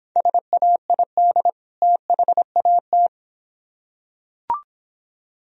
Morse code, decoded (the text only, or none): SAID THAT